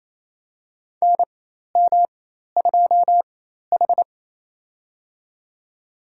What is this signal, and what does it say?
Morse code, heard as a keyed tone, about 27 wpm, spaced as for 13 wpm, 700 Hz. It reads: NM2H
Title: USA Call Signs - 4 Characters - 4x Character Spacing - 28wpm